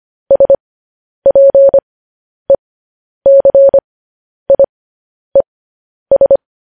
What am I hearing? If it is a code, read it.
Morse code, heard as a keyed tone, about 25 wpm, spaced as for 10 wpm, 550 Hz. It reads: SPECIES
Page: news-headlines-in-morse-code-at-10-wpm